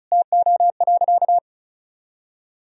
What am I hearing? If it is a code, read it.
Morse code, heard as a keyed tone, about 36 wpm, 700 Hz. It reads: TO.
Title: Sentences from Top 1000 Words - Review Part 2 - 35wpm